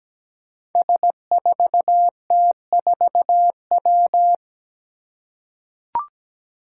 Morse code, heard as a keyed tone, about 17 wpm, 700 Hz. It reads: S4T4W
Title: Five Challenging Letter-Number Combinations 17wpm